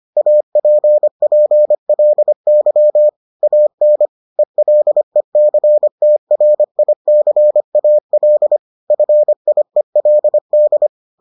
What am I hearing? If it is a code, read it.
Morse code, heard as a keyed tone, about 25 wpm, 600 Hz. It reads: APPLY AN ELECTRICAL FIELD